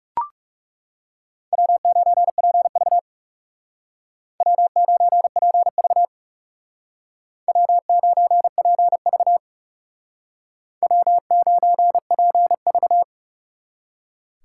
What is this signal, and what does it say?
Morse code, heard as a keyed tone, about 37 wpm, 700 Hz. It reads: W9PV W9PV W9PV W9PV
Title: Speed Racing - USA 1x2 Call Signs 30wpm